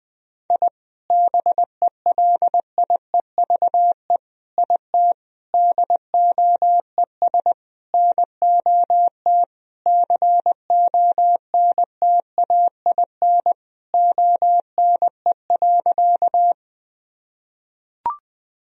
Morse code, heard as a keyed tone, about 20 wpm, 700 Hz. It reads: I BELIEVE IT DOES NOT CONTAIN ONE.
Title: Sentences from Top 1000 Words - Review Part 2 - 20wpm